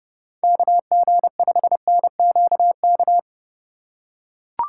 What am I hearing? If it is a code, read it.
Morse code, heard as a keyed tone, about 30 wpm, 700 Hz. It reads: KG5NQK
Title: USA Call Signs - 6 Characters - 2x Character Spacing - 30wpm